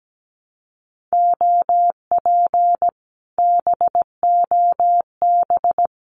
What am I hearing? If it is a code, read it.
Morse code, heard as a keyed tone, about 17 wpm, 700 Hz. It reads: OP BOB
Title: Common QSO Phrases Encore 17wpm